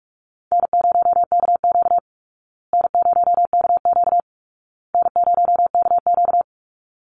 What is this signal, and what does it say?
Morse code, heard as a keyed tone, about 46 wpm, 700 Hz. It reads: N0KQ N0KQ N0KQ
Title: Easy Call Signs Repeated 3x - 45wpm